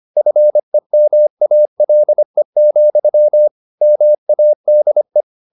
Morse code, heard as a keyed tone, about 25 wpm, 600 Hz. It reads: FEMALE, MADE
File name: the old man and the sea part 1 25wpm